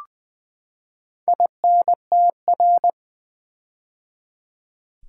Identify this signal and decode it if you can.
Morse code, heard as a keyed tone, about 20 wpm, 700 Hz. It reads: INTR
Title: Build A Word - 4 Letters At A Time - 12-Letter Words - 20wpm